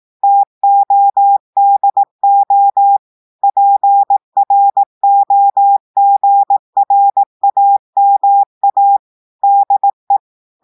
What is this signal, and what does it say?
Morse code, heard as a keyed tone, about 18 wpm, 800 Hz. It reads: TODO PROGRAMA DE
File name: audios / portugais / portugais texte en clair 2 vitesse 18 mots minutes